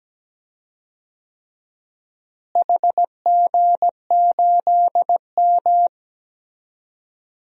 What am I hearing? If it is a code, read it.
Morse code, heard as a keyed tone, about 17 wpm, 700 Hz. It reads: HG8M
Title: DXCC-100 Call Signs - 4 Characters - Repeated 3x - 17wpm